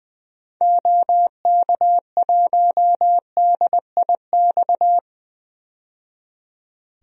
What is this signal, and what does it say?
Morse code, heard as a keyed tone, about 20 wpm, 700 Hz. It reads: OK1DIX
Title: DXCC-100 Call Signs - 6 Characters - 20wpm